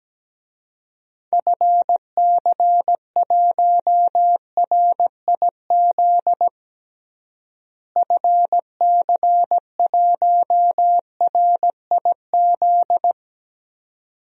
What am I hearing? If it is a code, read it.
Morse code, heard as a keyed tone, about 17 wpm, 700 Hz. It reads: FC1RIZ FC1RIZ
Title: DXCC-100 Call Signs - 6 Characters - Repeated 2x - 17wpm